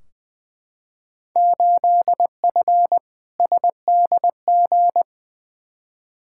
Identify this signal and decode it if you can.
Morse code, heard as a keyed tone, about 21 wpm, 700 Hz. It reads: 8F SDG
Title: Field Day Exchanges Encore - 20wpm